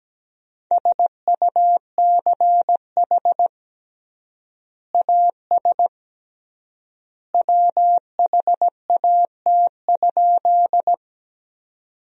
Morse code, heard as a keyed tone, about 17 wpm, 700 Hz. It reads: SUCH AS WHAT?
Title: 3x Word-Spacing - Sentences from Top 200 Words - 17wpm